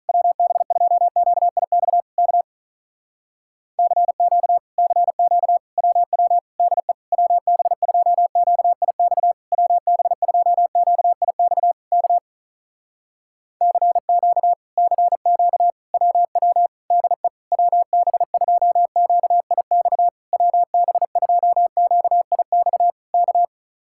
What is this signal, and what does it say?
Morse code, heard as a keyed tone, about 43 wpm, 700 Hz. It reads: WB2QIX K CQ CQ WW DE WB2QIX WB2QIX K CQ CQ WW DE WB2QIX WB2QIX K
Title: Speed-Racing Common QSO Phrases - 35wpm